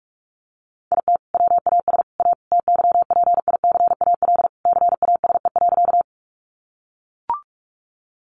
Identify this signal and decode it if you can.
Morse code, heard as a keyed tone, about 44 wpm, 700 Hz. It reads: IT WAS A TYPICAL CASE.
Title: Sentences from Top 1100 Words 45wpm